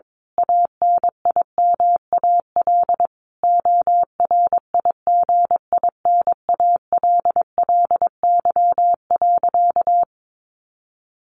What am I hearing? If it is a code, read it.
Morse code, heard as a keyed tone, about 22 wpm, 700 Hz. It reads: ANIMAL ORIGINALLY.
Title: A Sound Of Thunder 22wpm